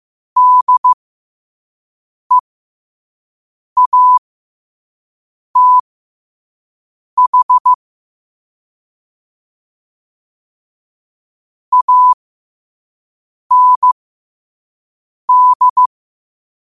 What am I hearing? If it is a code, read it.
Morse code, heard as a keyed tone, about 15 wpm, spaced as for 5 wpm, 1000 Hz. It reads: DEATH AND
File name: qotdfiles / 5wpm